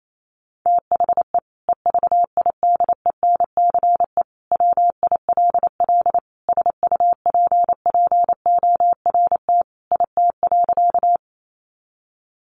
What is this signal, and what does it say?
Morse code, heard as a keyed tone, about 28 wpm, 700 Hz. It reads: THE EVIDENCE WILL SUPPORT IT.
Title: Sentences from Top 1300 Words 28wpm